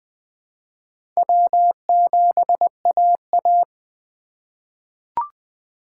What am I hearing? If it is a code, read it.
Morse code, heard as a keyed tone, about 20 wpm, 700 Hz. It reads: W7AA
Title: USA Call Signs - 4 Characters - 2x Character Spacing - 20wpm